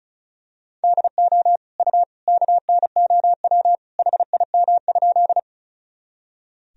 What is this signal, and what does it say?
Morse code, heard as a keyed tone, about 36 wpm, 700 Hz. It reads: DO U KNOW HIM?
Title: Sentences from Top 100 Words and Abbreviations 35wpm